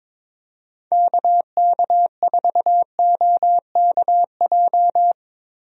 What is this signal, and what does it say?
Morse code, heard as a keyed tone, about 22 wpm, 700 Hz. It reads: KK4OKJ
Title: USA Call Signs - 6 Characters - Repeated 2x - 22wpm